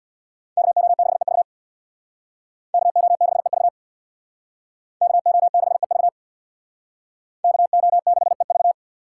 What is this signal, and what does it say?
Morse code, heard as a keyed tone, about 62 wpm, 700 Hz. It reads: KO6EV KO6EV KO6EV KO6EV
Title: Speed Racing - USA 2x2 Call Signs 50wpm